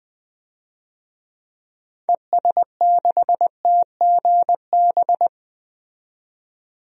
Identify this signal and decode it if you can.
Morse code, heard as a keyed tone, about 20 wpm, 700 Hz. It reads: ES6TGB
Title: DXCC-100 Call Signs - 6 Characters - 20wpm